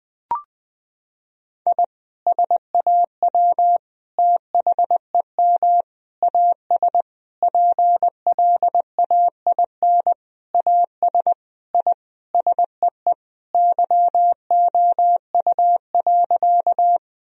Morse code, heard as a keyed tone, about 19 wpm, 700 Hz. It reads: I SAW THEM AS PLAIN AS I SEE YOU.
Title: Sentences from Top 1000 Words - Review Part 2 - 20wpm